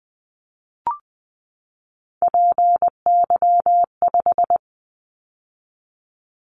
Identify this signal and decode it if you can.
Morse code, heard as a keyed tone, about 20 wpm, 700 Hz. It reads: PY5
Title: Callsign Prefixes for Top 50 DXCC - 20wpm